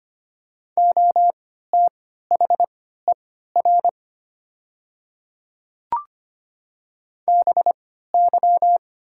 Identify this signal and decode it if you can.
Morse code, heard as a keyed tone, about 25 wpm, spaced as for 14 wpm, 700 Hz. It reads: OTHER BY
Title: 3x Character Spacing - Top 100 Words - 25wpm